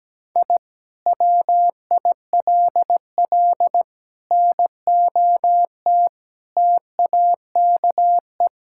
Morse code, heard as a keyed tone, about 17 wpm, 700 Hz. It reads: I WILL NOT TAKE
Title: Sentences from Top 1000 Words - Review Part 6 - 17wpm